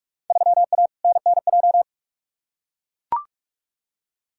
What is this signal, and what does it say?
Morse code, heard as a keyed tone, about 41 wpm, 700 Hz. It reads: 3A NNJ
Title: Field Day Exchanges - 45wpm